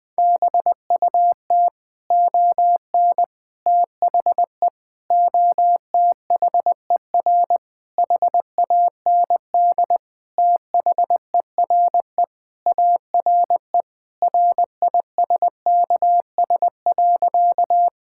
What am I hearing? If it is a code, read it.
Morse code, heard as a keyed tone, about 20 wpm, 700 Hz. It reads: BUT ON THE OTHER HAND THERE ARE RISKS.
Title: Sentences with 5-Word Phrases - 20wpm